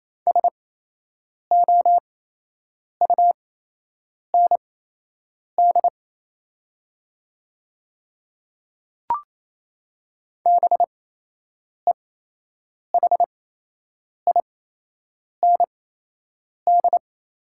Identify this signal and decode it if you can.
Morse code, heard as a keyed tone, about 28 wpm, spaced as for 8 wpm, 700 Hz. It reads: SOUND BEHIND
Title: 8x Character Spacing - Top 500 Words - 28wpm